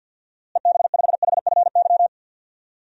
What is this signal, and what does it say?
Morse code, heard as a keyed tone, about 47 wpm, 700 Hz. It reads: EB5HFY
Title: DXCC-100 Call Signs - 6 Characters - Repeated 2x - 50wpm